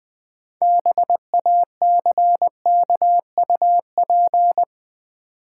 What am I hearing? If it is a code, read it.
Morse code, heard as a keyed tone, about 20 wpm, 700 Hz. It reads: BACKUP
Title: Build A Word - 3 Letters At A Time - 6-Letter Words - 20wpm